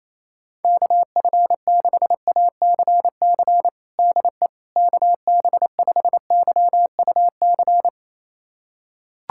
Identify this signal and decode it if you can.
Morse code, heard as a keyed tone, about 28 wpm, 700 Hz. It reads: KF6ACC DE KB5YUC
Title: USA Callsign DE Callsign 28wpm